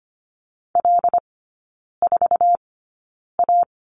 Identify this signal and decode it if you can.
Morse code, heard as a keyed tone, about 25 wpm, spaced as for 9 wpm, 690 Hz. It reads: L4A
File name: Lesson 14e - Number 2 - Three Characters 25wpm